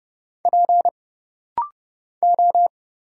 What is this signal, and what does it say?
Morse code, heard as a keyed tone, about 30 wpm, 700 Hz. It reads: P O
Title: Single Letters - Rapid-Fire 30wpm